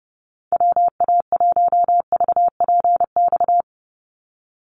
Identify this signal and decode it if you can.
Morse code, heard as a keyed tone, about 30 wpm, 700 Hz. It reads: WA1VPX
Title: USA Call Signs 30wpm